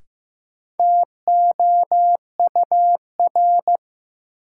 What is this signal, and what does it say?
Morse code, heard as a keyed tone, about 16 wpm, 700 Hz. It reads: TOUR
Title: Sentences from Top 1400 Words - New Word Drill Alt - 15wpm